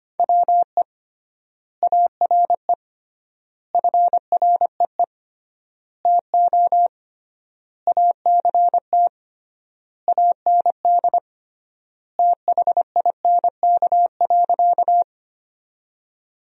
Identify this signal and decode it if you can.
Morse code, heard as a keyed tone, about 25 wpm, 700 Hz. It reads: WE ARE FREE TO ACT AND THINK.
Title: 3x Word-Spacing - Sentences from Top 500 Words - 25wpm